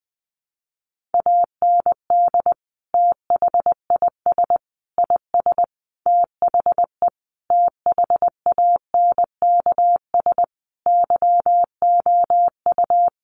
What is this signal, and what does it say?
Morse code, heard as a keyed tone, about 20 wpm, 700 Hz. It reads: AND THIS IS THE THANKS YOU
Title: Sentences from Top 1200 Words 20wpm